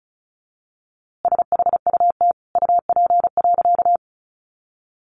Morse code, heard as a keyed tone, about 34 wpm, 700 Hz. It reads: SHUT UP.
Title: A Sound Of Thunder 35wpm